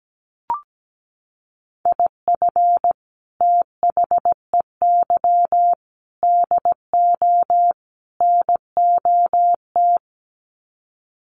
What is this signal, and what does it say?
Morse code, heard as a keyed tone, about 17 wpm, 700 Hz. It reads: IF THEY DO NOT
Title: Sets of 4 Words 17wpm